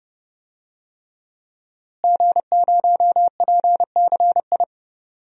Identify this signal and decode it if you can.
Morse code, heard as a keyed tone, about 30 wpm, 700 Hz. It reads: G0PCI
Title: DXCC-100 Call Signs - 5 Characters - Repeated 3x - 30wpm